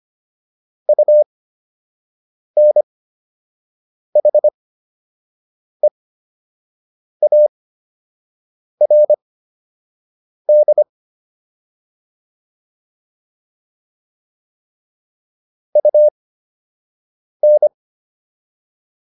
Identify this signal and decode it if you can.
Morse code, heard as a keyed tone, about 25 wpm, spaced as for 6 wpm, 600 Hz. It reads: UNHEARD UN